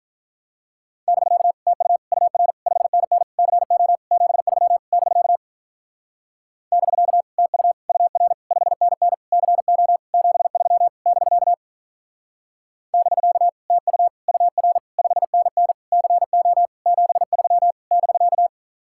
Morse code, heard as a keyed tone, about 43 wpm, 700 Hz. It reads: <BK> TU UR 5NN CO 73 <BK> <BK> TU UR 5NN CO 73 <BK> <BK> TU UR 5NN CO 73 <BK>
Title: Speed-Racing Common QSO Phrases Encore - 35wpm